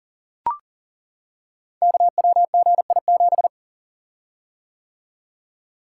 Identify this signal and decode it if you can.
Morse code, heard as a keyed tone, about 42 wpm, 700 Hz. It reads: KWGI7
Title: Code Groups - 5 Letter:Numbers - 40wpm